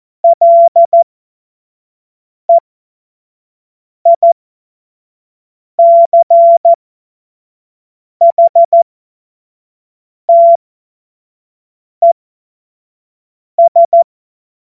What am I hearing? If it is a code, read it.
Morse code, heard as a keyed tone, about 14 wpm, spaced as for 5 wpm, 680 Hz. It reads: LEICHTES
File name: Morsekurs / Witze / Witz 30 Januar 2024 25 F70